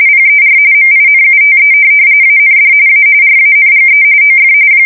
RTTY, about 45.45 baud. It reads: PER MINUTE AND A RANGE OF
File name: fsk170 / USS-New-Jersey